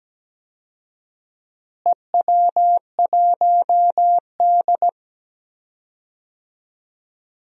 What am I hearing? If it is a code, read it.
Morse code, heard as a keyed tone, about 17 wpm, 700 Hz. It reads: EW1D
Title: DXCC-100 Call Signs - 4 Characters - Repeated 3x - 17wpm